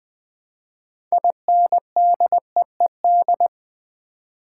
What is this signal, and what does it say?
Morse code, heard as a keyed tone, about 20 wpm, 700 Hz. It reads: INDEED
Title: Build A Word - 2 Letters At A Time - 6-Letter Words - 20wpm